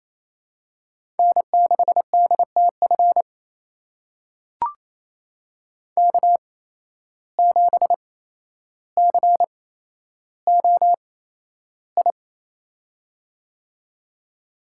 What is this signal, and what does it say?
Morse code, heard as a keyed tone, about 27 wpm, spaced as for 12 wpm, 700 Hz. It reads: N6DTF K7COI
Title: USA Call Signs - 5 Characters - 8x Character Spacing - 28wpm